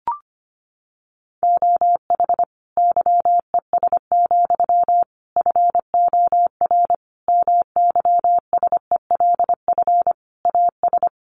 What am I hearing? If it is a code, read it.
Morse code, heard as a keyed tone, about 25 wpm, 700 Hz. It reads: OH YES, FOR MYSELF AS
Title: Sentences from Top 1300 Words 25wpm